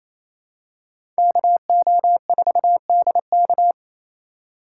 Morse code, heard as a keyed tone, about 28 wpm, 700 Hz. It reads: KO4DK
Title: USA Call Signs - 5 Characters - 4x Character Spacing - 28wpm